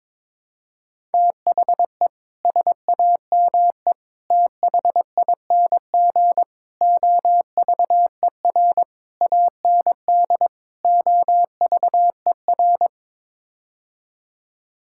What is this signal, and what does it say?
Morse code, heard as a keyed tone, about 22 wpm, 700 Hz. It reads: THE SAME THING OVER AND OVER
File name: practice / 22wpm / Sets of 6 Words - 22wpm